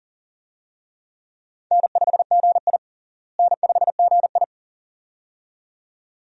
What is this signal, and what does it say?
Morse code, heard as a keyed tone, about 40 wpm, 700 Hz. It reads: N5GI N5GI
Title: USA Call Signs - 4 Characters - Repeated 2x - 40wpm